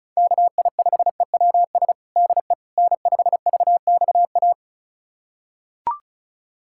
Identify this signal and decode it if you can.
Morse code, heard as a keyed tone, about 36 wpm, 700 Hz. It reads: KI5EWS DE N5VXA
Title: USA Callsign DE Callsign 35wpm